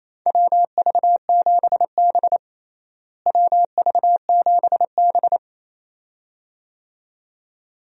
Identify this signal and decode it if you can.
Morse code, heard as a keyed tone, about 27 wpm, 700 Hz. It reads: WV7B WV7B
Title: USA Call Signs - 4 Characters - Repeated 2x - 28wpm